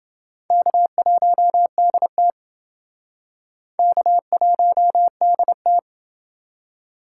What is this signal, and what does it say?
Morse code, heard as a keyed tone, about 28 wpm, 700 Hz. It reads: K1DT K1DT